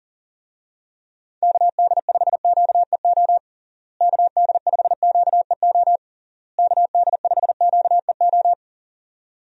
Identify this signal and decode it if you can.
Morse code, heard as a keyed tone, about 40 wpm, 700 Hz. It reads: KD5QEO KD5QEO KD5QEO
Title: USA Call Signs - 6 Characters - Repeated 3x - 40wpm